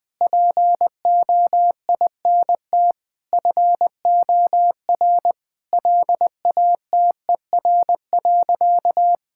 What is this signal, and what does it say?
Morse code, heard as a keyed tone, about 20 wpm, 700 Hz. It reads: POINT FOR LATER.
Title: Sentences from Top 1000 Words - Review Part 6 - 20wpm